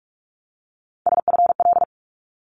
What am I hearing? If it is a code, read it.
Morse code, heard as a keyed tone, about 46 wpm, 700 Hz. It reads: SFL
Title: ARRL Sections 45wpm